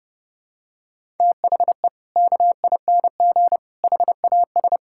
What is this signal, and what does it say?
Morse code, heard as a keyed tone, about 30 wpm, 700 Hz. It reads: THE KING HAS
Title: Sentences from Top 1000 Words - Review Part 6 - 30wpm